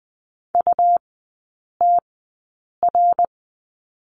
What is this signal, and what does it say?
Morse code, heard as a keyed tone, about 20 wpm, spaced as for 8 wpm, 690 Hz. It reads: UTR
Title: 166 Lesson 27e - Number 7 - Three Characters 20wpm